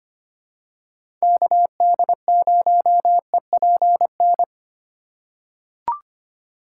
Morse code, heard as a keyed tone, about 25 wpm, 700 Hz. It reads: KD0EPN
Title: USA Call Signs - 6 Characters - 4x Character Spacing - 25wpm